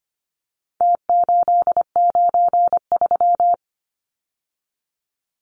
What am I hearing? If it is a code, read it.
Morse code, heard as a keyed tone, about 25 wpm, 700 Hz. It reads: T893
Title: Numbers - Contest Style - 4-Digit 25wpm